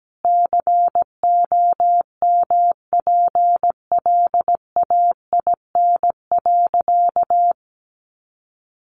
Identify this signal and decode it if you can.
Morse code, heard as a keyed tone, about 17 wpm, 700 Hz. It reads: COMPLAIN.
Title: Sentences from Top 1300 Words 17wpm